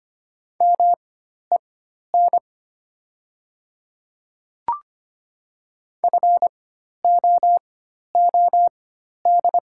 Morse code, heard as a keyed tone, about 25 wpm, spaced as for 12 wpm, 700 Hz. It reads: MEN FOOD
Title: 4x Character Spacing - Top 300 Words - 25wpm